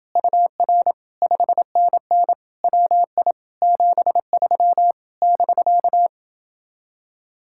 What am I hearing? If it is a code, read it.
Morse code, heard as a keyed tone, about 29 wpm, 700 Hz. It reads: UR 5NN WI 73 <BK>